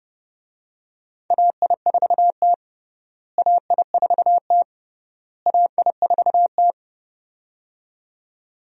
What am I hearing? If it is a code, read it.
Morse code, heard as a keyed tone, about 30 wpm, 700 Hz. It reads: AI4T AI4T AI4T
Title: USA Call Signs - 4 Characters - Repeated 3x - 30wpm